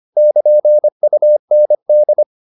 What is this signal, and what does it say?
Morse code, heard as a keyed tone, about 25 wpm, 600 Hz. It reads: <KN> UND